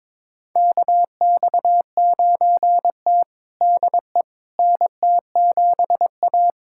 Morse code, heard as a keyed tone, about 22 wpm, 700 Hz. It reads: KX9T DE NT7A
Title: USA Callsign DE Callsign - 4 Characters - 22wpm